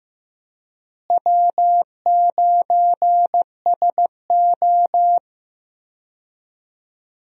Morse code, heard as a keyed tone, about 15 wpm, 700 Hz. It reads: W9SO
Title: USA Call Signs - 4 Characters - Repeated 3x - 15wpm